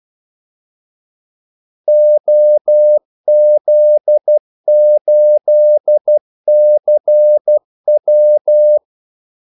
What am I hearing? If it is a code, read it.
Morse code, heard as a keyed tone, about 12 wpm, 600 Hz. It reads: OZ8CW
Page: 2 ekstra ordmellemrum mellem kaldesignalerne